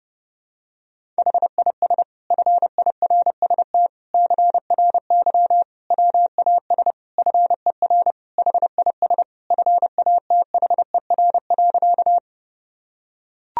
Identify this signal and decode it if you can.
Morse code, heard as a keyed tone, about 30 wpm, 700 Hz. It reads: HIS FIRST CRY WAS FER HIS FATHER.
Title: Sentences from Top 500 Words and Abbreviations 30wpm